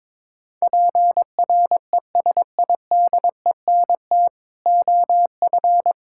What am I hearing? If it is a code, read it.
Morse code, heard as a keyed tone, about 22 wpm, 700 Hz. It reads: PRESIDENT OF
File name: practice / 22wpm / Build A Phrase - 5-Word Phrases - 22wpm